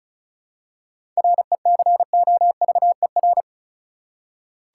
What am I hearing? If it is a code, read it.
Morse code, heard as a keyed tone, about 34 wpm, 700 Hz. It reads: RECOVER
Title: Seven Letter Words - 35wpm